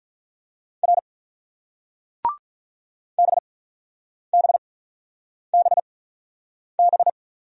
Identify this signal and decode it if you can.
Morse code, heard as a keyed tone, about 42 wpm, spaced as for 9 wpm, 700 Hz. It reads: R BBBB
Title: Speed-Racing Single Letters 35wpm